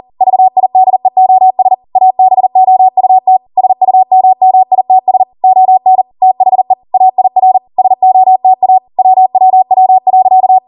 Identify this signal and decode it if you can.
Morse code, heard as a keyed tone, about 40 wpm, 750 Hz. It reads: VIDEOS ABOUT SUMMITS ON THE AIR SOTA WWW.